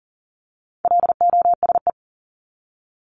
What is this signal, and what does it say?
Morse code, heard as a keyed tone, about 40 wpm, 700 Hz. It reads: LOSE
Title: Top 1001-1250 Words 40wpm